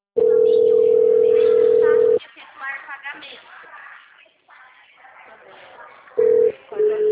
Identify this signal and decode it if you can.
Ringback tone (North America), 440 + 480 Hz, with speech.
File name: out- -2002- - - .11670